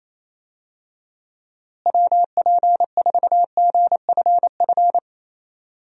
Morse code, heard as a keyed tone, about 28 wpm, 700 Hz. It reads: WP4GFF
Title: USA Call Signs - 6 Characters - Repeated 3x - 28wpm